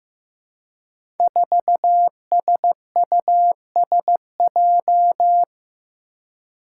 Morse code, heard as a keyed tone, about 15 wpm, 700 Hz. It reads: 4SUSJ
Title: Five Challenging Letter-Number Combinations 15wpm